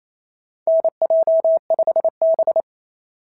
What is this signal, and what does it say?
Morse code, heard as a keyed tone, about 27 wpm, 690 Hz. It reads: NJ5B
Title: SOTA QSO Practice 28wpm